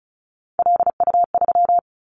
Morse code, heard as a keyed tone, about 36 wpm, 700 Hz. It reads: LU3
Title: Callsign Prefixes for Top 100 DXCC - 35wpm